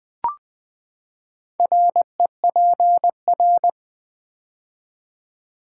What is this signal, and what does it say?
Morse code, heard as a keyed tone, about 20 wpm, 700 Hz. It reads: REPR
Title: Build A Word - 4 Letters At A Time - 12-Letter Words - 20wpm